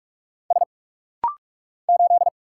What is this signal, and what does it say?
Morse code, heard as a keyed tone, about 44 wpm, 690 Hz. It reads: S 8
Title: Single Letter-Number - Rapid-Fire 45wpm